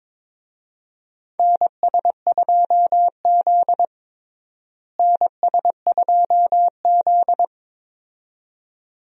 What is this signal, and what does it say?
Morse code, heard as a keyed tone, about 21 wpm, 700 Hz. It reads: NS2Z NS2Z
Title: USA Call Signs - 4 Characters - Repeated 2x - 22wpm